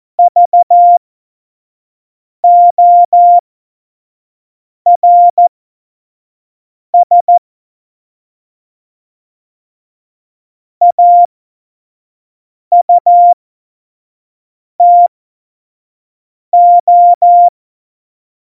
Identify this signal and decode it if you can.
Morse code, heard as a keyed tone, about 14 wpm, spaced as for 5 wpm, 700 Hz. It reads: VORS AUTO
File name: Morsekurs / Witze / Witz 02 Juni 2020, 25 F70